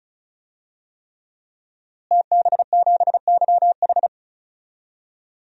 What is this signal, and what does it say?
Morse code, heard as a keyed tone, about 36 wpm, 700 Hz. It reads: TB7YH
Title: DXCC-100 Call Signs - 5 Characters - Repeated 3x - 35wpm